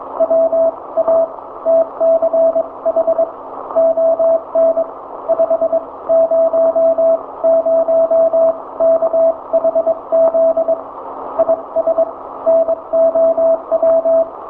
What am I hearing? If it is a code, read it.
Morse code, heard as a keyed tone, about 22 wpm, spaced as for 18 wpm, 660 Hz. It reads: WATCH ON 500KHZ IS NOW
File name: hamradio / last 500khz cw